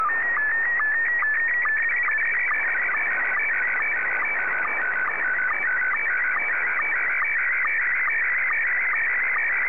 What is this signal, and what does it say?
SSTV (slow-scan television), Scottie 1.
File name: sstv / S1-VK7NMH